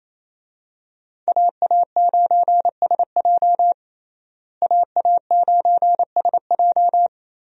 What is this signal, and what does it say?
Morse code, heard as a keyed tone, about 28 wpm, 700 Hz. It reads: AA9SJ AA9SJ
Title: USA Call Signs - 5 Characters - Repeated 3x - 28wpm